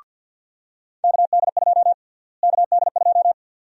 Morse code, heard as a keyed tone, about 52 wpm, 700 Hz. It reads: KD2 KD2
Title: USA Call Sign Prefixes - Repeated 2x - 50wpm